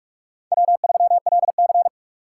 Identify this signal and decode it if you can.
Morse code, heard as a keyed tone, about 43 wpm, 700 Hz. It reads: W3LC
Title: USA Call Signs - 4x Character Spacing - 45wpm